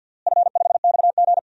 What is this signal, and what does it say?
Morse code, heard as a keyed tone, about 50 wpm, 700 Hz. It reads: F5XG
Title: DXCC-100 Call Signs - 4 Characters - Repeated 3x - 50wpm